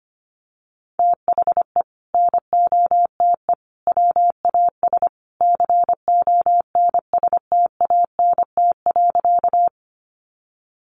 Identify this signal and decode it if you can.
Morse code, heard as a keyed tone, about 25 wpm, 700 Hz. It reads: THE NOTE WAS CONSTANT.
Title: Sentences from Top 1500 Words 25wpm